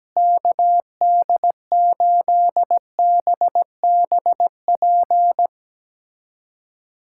Morse code, heard as a keyed tone, about 17 wpm, 700 Hz. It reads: KD8BBP
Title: USA Callsign DE Callsign 17wpm